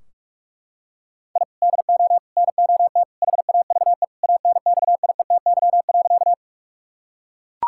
Morse code, heard as a keyed tone, about 41 wpm, 700 Hz. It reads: I DO NOT HAVE ANXIETY.
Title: Sentences from Top 1600 Words 45wpm